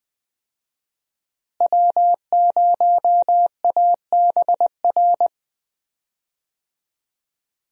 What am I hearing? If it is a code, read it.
Morse code, heard as a keyed tone, about 20 wpm, 700 Hz. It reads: W0ABR
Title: USA Call Signs - 5 Characters - Repeated 3x - 20wpm